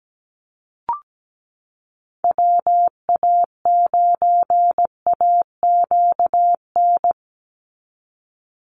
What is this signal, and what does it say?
Morse code, heard as a keyed tone, about 17 wpm, 700 Hz. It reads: WA9AQN
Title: Call Signs - 17wpm